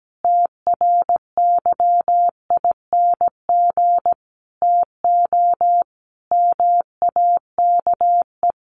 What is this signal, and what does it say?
Morse code, heard as a keyed tone, about 17 wpm, 700 Hz. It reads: TRYING TO MAKE
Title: A Sound Of Thunder 17wpm